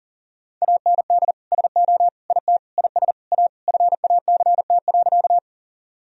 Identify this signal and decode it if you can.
Morse code, heard as a keyed tone, about 40 wpm, 700 Hz. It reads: AND SO IT IS A FACT.
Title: Sentences from Top 1000 Words - Review Part 3 - 40wpm